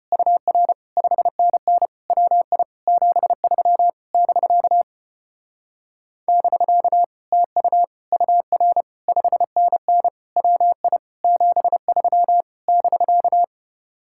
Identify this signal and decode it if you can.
Morse code, heard as a keyed tone, about 33 wpm, 700 Hz. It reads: UR 5NN WI 73 <BK> <BK> TU UR 5NN WI 73 <BK>